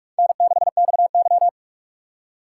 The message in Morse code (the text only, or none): N6XY